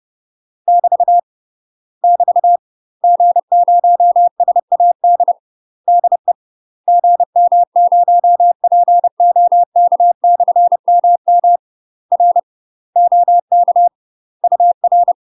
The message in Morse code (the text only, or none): <BT> <BT> G0SAD DE GM0POK/MM R OK UR